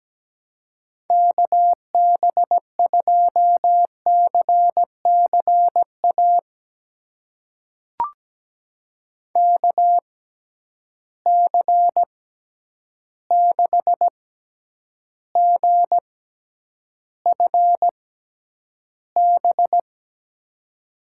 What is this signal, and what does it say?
Morse code, heard as a keyed tone, about 17 wpm, spaced as for 9 wpm, 700 Hz. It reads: KB2CCA KC6GFB
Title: USA Call Signs - 6 Characters - 6x Character Spacing - 17wpm